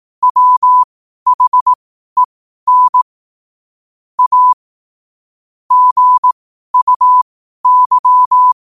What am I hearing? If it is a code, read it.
Morse code, heard as a keyed tone, about 18 wpm, spaced as for 12 wpm, 1000 Hz. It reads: WHEN A GUY
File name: qotdfiles / 12wpm